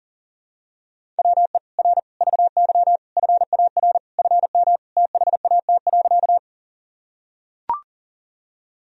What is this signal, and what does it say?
Morse code, heard as a keyed tone, about 39 wpm, 700 Hz. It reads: WE R VY FAR FM THAT.
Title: Sentences from Top 300 Words and Abbreviations 40wpm